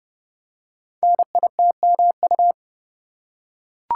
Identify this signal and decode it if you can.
Morse code, heard as a keyed tone, about 30 wpm, 700 Hz. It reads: NITMU
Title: Code Groups - 5 Letter:Numbers - 30wpm